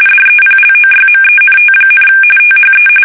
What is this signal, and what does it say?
RTTY, about 45.45 baud. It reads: DEPLOYMENT, THE
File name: fsk850 / battleshipnj